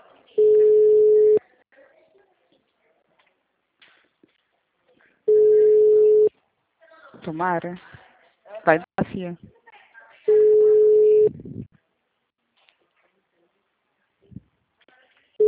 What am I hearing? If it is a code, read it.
Ringback tone (Europe, 425 Hz), with speech.